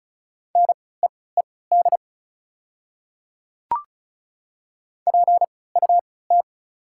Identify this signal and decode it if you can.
Morse code, heard as a keyed tone, about 36 wpm, spaced as for 20 wpm, 700 Hz. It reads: NEED PUT
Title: 3x Character Spacing - Top 200 Words - 35wpm